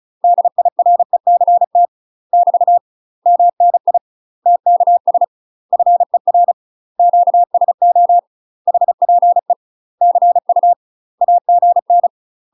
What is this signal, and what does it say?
Morse code, heard as a keyed tone, about 36 wpm, 700 Hz. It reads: DIRECT <BT> MNI TKS FER QSO HPE CU AGN
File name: UK MASTER QSOs TRAINING/35WPM